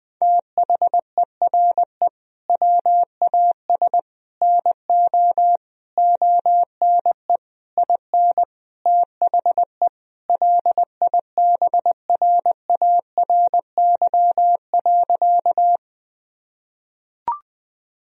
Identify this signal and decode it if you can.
Morse code, heard as a keyed tone, about 20 wpm, 700 Hz. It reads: THERE WAS NO ONE IN THE LIBRARY.
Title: Sentences from Top 1000 Words - Review Part 4 - 20wpm